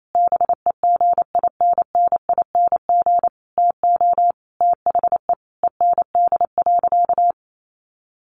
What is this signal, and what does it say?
Morse code, heard as a keyed tone, about 28 wpm, 700 Hz. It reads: BEGINNING TO THE END.
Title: Sentences from Top 1300 Words 28wpm